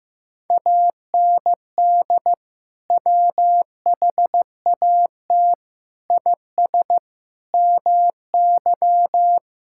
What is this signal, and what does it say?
Morse code, heard as a keyed tone, about 15 wpm, 700 Hz. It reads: AND WHAT IS MY
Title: Sentences from Top 1000 Words - Review Part 6 - 15wpm